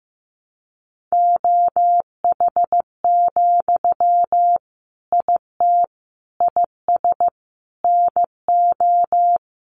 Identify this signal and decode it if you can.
Morse code, heard as a keyed tone, about 15 wpm, 700 Hz. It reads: OH, IT IS NO
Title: Sentences from Top 1200 Words 15wpm